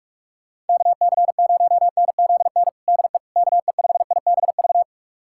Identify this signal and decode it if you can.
Morse code, heard as a keyed tone, about 46 wpm, 700 Hz. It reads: KC0NZN DE KE5IBV
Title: USA Callsign DE Callsign 45wpm